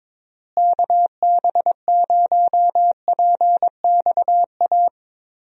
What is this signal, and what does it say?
Morse code, heard as a keyed tone, about 22 wpm, 700 Hz. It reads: KB0PXA
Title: USA Call Signs - 6 Characters - Repeated 3x - 22wpm